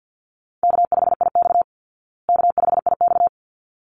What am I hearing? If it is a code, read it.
Morse code, heard as a keyed tone, about 50 wpm, 700 Hz. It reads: K5IX K5IX
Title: Call Signs Repeated 3x - 50wpm